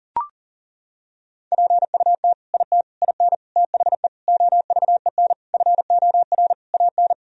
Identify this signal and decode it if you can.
Morse code, heard as a keyed tone, about 37 wpm, 700 Hz. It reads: PUT IT IN THE OVEN FOR AN
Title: Sentences from Top 1000 Words - Review Part 4 - 40wpm